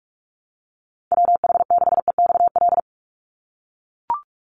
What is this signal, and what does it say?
Morse code, heard as a keyed tone, about 44 wpm, 700 Hz. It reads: WH6EXL
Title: USA Call Signs 45wpm